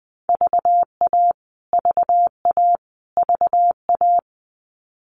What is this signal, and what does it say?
Morse code, heard as a keyed tone, about 20 wpm, 700 Hz. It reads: VA VA VA
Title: US States Repeated 3x 20wpm